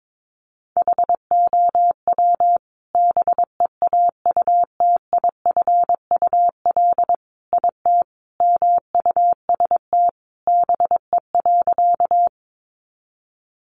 Morse code, heard as a keyed tone, about 22 wpm, 700 Hz. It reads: HOW BEAUTIFUL IT MUST BE.
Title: Sentences from Top 900 Words 22wpm